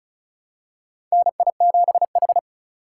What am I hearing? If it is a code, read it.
Morse code, heard as a keyed tone, about 34 wpm, 700 Hz. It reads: NI7H
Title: USA Call Signs - 4 Characters - 3x Character Spacing - 35wpm